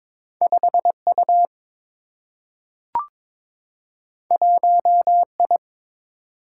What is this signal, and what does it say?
Morse code, heard as a keyed tone, about 22 wpm, 700 Hz. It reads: 5 U 1 I
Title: Single-Letter Single-Number 22wpm